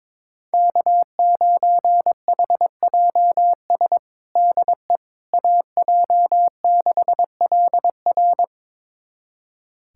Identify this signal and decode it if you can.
Morse code, heard as a keyed tone, about 21 wpm, 700 Hz. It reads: K9HJS DE AJ6LR
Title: USA Callsign DE Callsign - 5 Characters - 22wpm